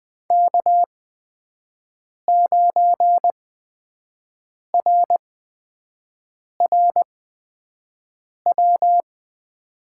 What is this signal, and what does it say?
Morse code, heard as a keyed tone, about 20 wpm, spaced as for 5 wpm, 700 Hz. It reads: K9RRW
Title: USA Call Signs - 5 Characters - 8x Character Spacing - 20wpm